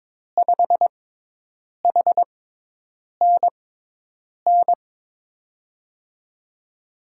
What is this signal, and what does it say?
Morse code, heard as a keyed tone, about 21 wpm, spaced as for 8 wpm, 700 Hz. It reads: 5HNN
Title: USA Call Signs - 6 Characters - 6x Character Spacing - 22wpm